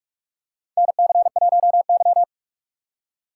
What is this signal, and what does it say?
Morse code, heard as a keyed tone, about 46 wpm, 700 Hz. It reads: NC1Y
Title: USA Call Signs - 4 Characters - 4x Character Spacing - 45wpm